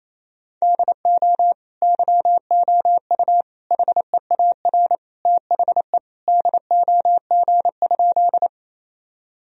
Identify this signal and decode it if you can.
Morse code, heard as a keyed tone, about 28 wpm, 700 Hz. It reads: DO YOU HEAR THE DOG?
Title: Sentences from Top 1000 Words - Review Part 2 - 28wpm